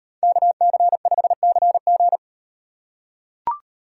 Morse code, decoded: KC5CG

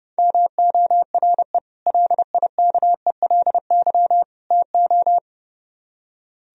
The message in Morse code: MORE LIKELY TO

